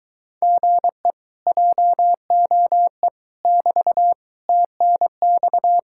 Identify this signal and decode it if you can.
Morse code, heard as a keyed tone, about 24 wpm, 700 Hz. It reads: GE JOE <BT> TNX